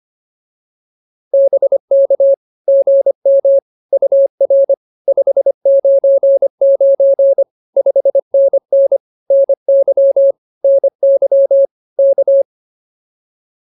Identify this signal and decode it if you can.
Morse code, heard as a keyed tone, about 24 wpm, 550 Hz. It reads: BK GM UR 599 5NN NY NY K